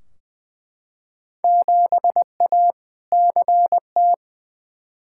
Morse code, heard as a keyed tone, about 20 wpm, 700 Hz. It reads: 7A CT
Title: Field Day Exchanges - 20wpm